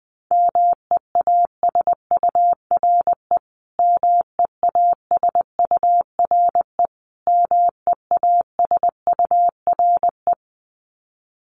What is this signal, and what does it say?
Morse code, heard as a keyed tone, about 19 wpm, 700 Hz. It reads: MEASURE MEASURE MEASURE
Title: Top 500 Words Repeated 20wpm